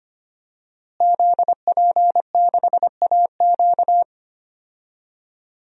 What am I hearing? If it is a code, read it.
Morse code, heard as a keyed tone, about 25 wpm, 700 Hz. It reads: ZP6AQ
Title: DXCC-100 Call Signs - 5 Characters - Repeated 2x - 25wpm